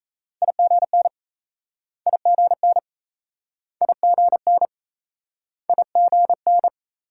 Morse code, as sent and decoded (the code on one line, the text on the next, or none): .. --. -. / .. --. -. / .. --. -. / .. --. -.
IGN IGN IGN IGN